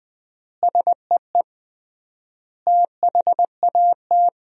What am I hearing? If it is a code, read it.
Morse code, heard as a keyed tone, about 21 wpm, 700 Hz. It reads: SEE THAT
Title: 3x Word-Spacing - Sentences from Top 300 Words - 20wpm